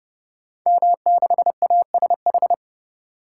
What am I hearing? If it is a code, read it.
Morse code, heard as a keyed tone, about 30 wpm, 700 Hz. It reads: M6ASH
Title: Code Groups - 5 Letter:Numbers - 30wpm